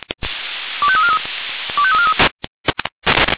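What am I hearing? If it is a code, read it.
Five-tone selective call (ZVEI), heard as quick runs of five tones, tones of about 70 ms. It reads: 25343 25343
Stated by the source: Five-tone ZVEI selective call received on an SDR.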